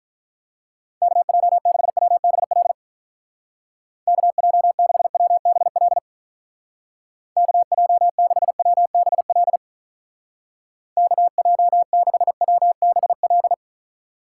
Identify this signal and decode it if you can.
Morse code, heard as a keyed tone, about 44 wpm, 700 Hz. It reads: KJ6WBL KJ6WBL KJ6WBL KJ6WBL
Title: Speed Racing - USA Call Signs 35wpm